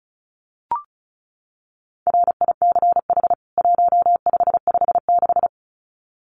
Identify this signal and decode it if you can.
Morse code, heard as a keyed tone, about 34 wpm, 700 Hz. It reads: RICH 1556
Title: CWT Exchange 35wpm